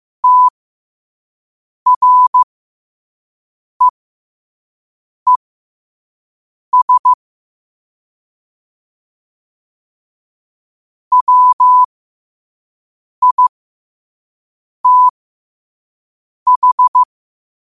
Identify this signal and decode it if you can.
Morse code, heard as a keyed tone, about 15 wpm, spaced as for 5 wpm, 1000 Hz. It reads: TREES WITH